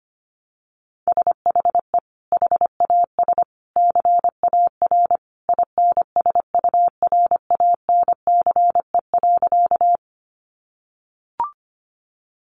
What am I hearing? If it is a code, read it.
Morse code, heard as a keyed tone, about 25 wpm, 700 Hz. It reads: SHE HAS CAR INSURANCE.
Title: Sentences from Top 1100 Words 25wpm